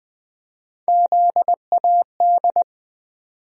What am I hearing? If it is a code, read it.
Morse code, heard as a keyed tone, about 20 wpm, 700 Hz. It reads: ZAD
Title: USA Call Sign Postfixes - 3 Letters 20wpm